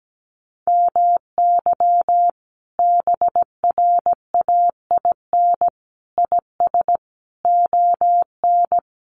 Morse code, heard as a keyed tone, about 17 wpm, 700 Hz. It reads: MY BRAIN IS ON
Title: Sentences from Top 1100 Words 17wpm